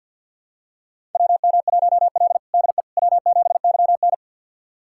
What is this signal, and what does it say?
Morse code, heard as a keyed tone, about 52 wpm, 700 Hz. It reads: WM1L DE W7YN
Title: USA Callsign DE Callsign - 4 Characters - 50wpm